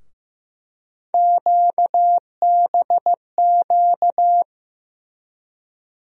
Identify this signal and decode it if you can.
Morse code, heard as a keyed tone, about 15 wpm, 700 Hz. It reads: QBQ
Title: USA Call Sign Postfixes - 3 Letters 15wpm